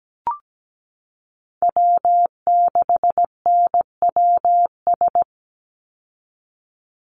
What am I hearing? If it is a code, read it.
Morse code, heard as a keyed tone, about 17 wpm, 700 Hz. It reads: W6NWS
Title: Call Signs Repeated 2x - 17wpm